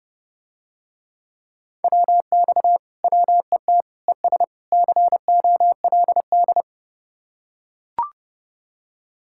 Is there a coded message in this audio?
Yes: Morse code from 1.8 to 6.6 s, heard as a keyed tone, about 30 wpm, 700 Hz. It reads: WX WET ES COLD